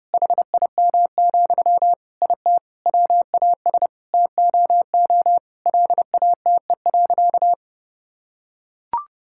Morse code, heard as a keyed tone, about 32 wpm, 700 Hz. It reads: HIM, IT WAS TOO LATE.
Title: Sentences from Top 1000 Words - Review Part 6 - 30wpm